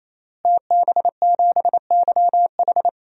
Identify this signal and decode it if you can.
Morse code, heard as a keyed tone, about 28 wpm, 700 Hz. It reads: TB7YH
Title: DXCC-100 Call Signs - 5 Characters - Repeated 3x - 28wpm